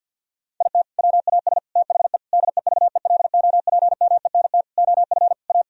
Morse code, heard as a keyed tone, about 50 wpm, 700 Hz. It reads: IT WAS THE DEVELOPMENT OF A
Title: Sentences from Top 1000 Words - Review Part 4 - 50wpm